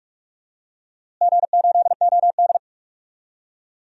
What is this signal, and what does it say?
Morse code, heard as a keyed tone, about 46 wpm, 700 Hz. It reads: G8OD
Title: DXCC-100 Call Signs - 4 Characters - Repeated 3x - 45wpm